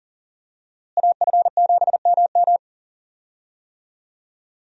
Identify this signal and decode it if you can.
Morse code, heard as a keyed tone, about 40 wpm, 700 Hz. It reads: AF7MM
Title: USA Call Signs - 5 Characters - Repeated 2x - 40wpm